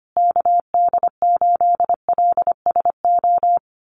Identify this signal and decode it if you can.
Morse code, heard as a keyed tone, about 25 wpm, 700 Hz. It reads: KD8LSO
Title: USA Call Signs Repeated 2x 25wpm